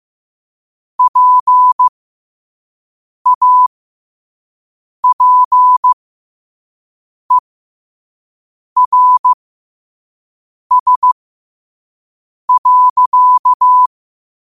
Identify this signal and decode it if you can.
Morse code, heard as a keyed tone, about 15 wpm, spaced as for 5 wpm, 1000 Hz. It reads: PAPERS.
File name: qotdfiles / 5wpm